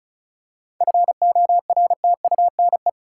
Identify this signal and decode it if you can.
Morse code, heard as a keyed tone, about 36 wpm, 700 Hz. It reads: FORTUNE
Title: Seven Letter Words - 35wpm